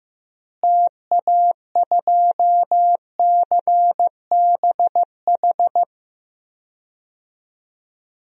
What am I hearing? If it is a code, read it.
Morse code, heard as a keyed tone, about 15 wpm, 700 Hz. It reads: TA2CBH
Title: DXCC-100 Call Signs - 6 Characters - Repeated 2x - 15wpm